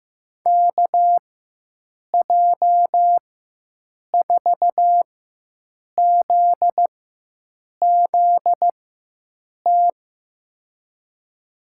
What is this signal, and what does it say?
Morse code, heard as a keyed tone, about 15 wpm, spaced as for 7 wpm, 700 Hz. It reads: KJ4ZZT
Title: USA Call Signs - 6 Characters - 4x Character Spacing - 15wpm